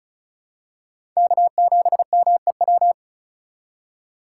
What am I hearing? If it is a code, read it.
Morse code, heard as a keyed tone, about 36 wpm, 690 Hz. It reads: K7MEW
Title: USA Call Signs - 5 Characters - 3x Character Spacing - 35wpm